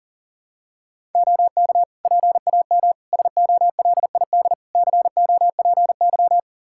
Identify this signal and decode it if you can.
Morse code, heard as a keyed tone, about 40 wpm, 700 Hz. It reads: OK PAM SOLID COPY
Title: K1USN SST QSO Practice 40wpm